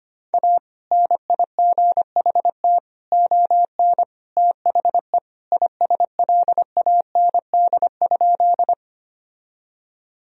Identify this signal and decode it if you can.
Morse code, heard as a keyed tone, about 25 wpm, 700 Hz. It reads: A NIGHT ON THE ISLAND?
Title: Sentences from Top 1000 Words - Review Part 3 - 25wpm